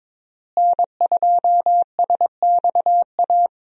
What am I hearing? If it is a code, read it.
Morse code, heard as a keyed tone, about 21 wpm, 700 Hz. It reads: N2SXA
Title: USA Call Signs - 5 Characters - Repeated 2x - 22wpm